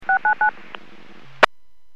DTMF, heard as telephone touch tones, tones of about 110 ms. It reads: 699